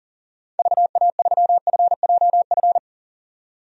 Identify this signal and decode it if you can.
Morse code, heard as a keyed tone, about 40 wpm, 700 Hz. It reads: VA3FJF
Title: DXCC-100 Call Signs - 6 Characters - 40wpm